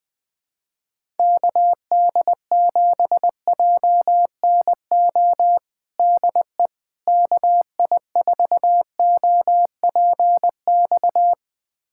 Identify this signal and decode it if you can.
Morse code, heard as a keyed tone, about 21 wpm, 700 Hz. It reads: KD7JNO DE KI4OPX
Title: USA Callsign DE Callsign 20wpm